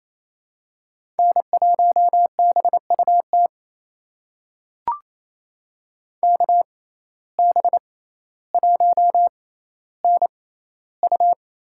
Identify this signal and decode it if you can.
Morse code, heard as a keyed tone, about 28 wpm, spaced as for 14 wpm, 700 Hz. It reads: N1BUT KB1NU
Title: USA Call Signs - 5 Characters - 6x Character Spacing - 28wpm